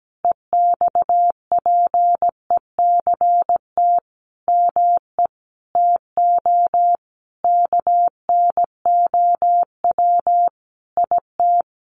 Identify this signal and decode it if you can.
Morse code, heard as a keyed tone, about 17 wpm, 700 Hz. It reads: EXPECT ME TO KNOW IT
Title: Sentences from Top 1100 Words 17wpm